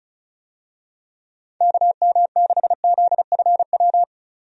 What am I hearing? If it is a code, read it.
Morse code, heard as a keyed tone, about 34 wpm, 700 Hz. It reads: KM6ZFW
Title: USA Call Signs - 6 Characters - 35wpm